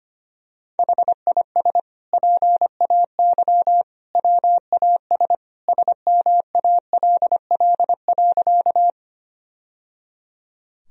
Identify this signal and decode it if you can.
Morse code, heard as a keyed tone, about 25 wpm, 700 Hz. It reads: HIS PAY WAS SMALL.
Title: Sentences from Top 1000 Words - Review Part 1 - 25wpm